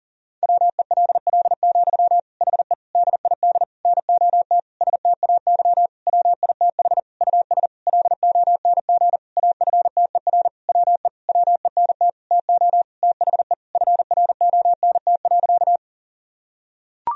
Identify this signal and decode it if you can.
Morse code, heard as a keyed tone, about 40 wpm, 700 Hz. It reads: WELL, HE DID NOT STAY WITH US LONG AFTER WE WENT TO THE FRONT.
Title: Sentences from Top 1000 Words - Review Part 2 - 40wpm